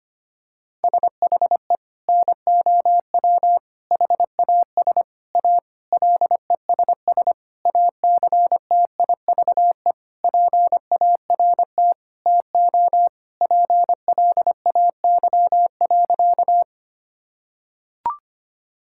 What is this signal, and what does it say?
Morse code, heard as a keyed tone, about 25 wpm, 700 Hz. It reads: SHE NOW HAS A LESS ACTIVE PART TO PLAY.
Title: Sentences from Top 1000 Words - Review Part 6 - 25wpm